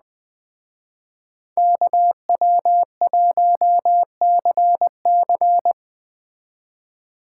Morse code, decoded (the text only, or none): KW1CC